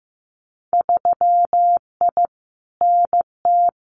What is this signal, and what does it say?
Morse code, heard as a keyed tone, about 15 wpm, 700 Hz. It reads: 3I NT
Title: Winter Field Day Exchanges Encore 15wpm